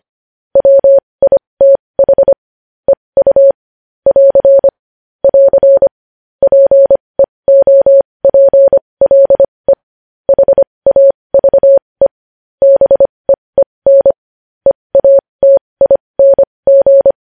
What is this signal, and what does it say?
Morse code, heard as a keyed tone, about 25 wpm, spaced as for 20 wpm, 550 Hz. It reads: WITH EU <AR> <AR> PEOPLE HAVE BEEN EATING
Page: News Headlines in Morse Code at 20 WPM